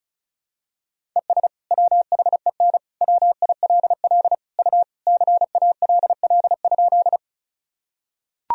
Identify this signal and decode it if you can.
Morse code, heard as a keyed tone, about 36 wpm, 700 Hz. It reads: ES WHEN WILL U CALL?
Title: Sentences from Top 100 Words and Abbreviations 35wpm